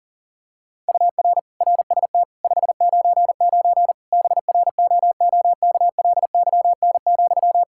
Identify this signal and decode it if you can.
Morse code, heard as a keyed tone, about 40 wpm, 700 Hz. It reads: UR RST 599 BROOKLYN,
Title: Practice QSO - Exchange 1 - Beginning - 40wpm